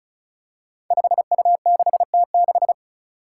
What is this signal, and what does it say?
Morse code, heard as a keyed tone, about 36 wpm, 700 Hz. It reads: 5U6T6
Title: Five Challenging Letter-Number Combinations 35wpm